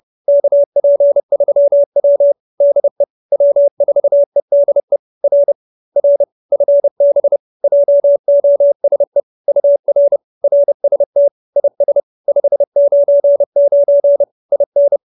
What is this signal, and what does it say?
Morse code, heard as a keyed tone, about 30 wpm, 570 Hz. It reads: KP3W DE W4EDE R R FB JOSE UR RST IS 599 IN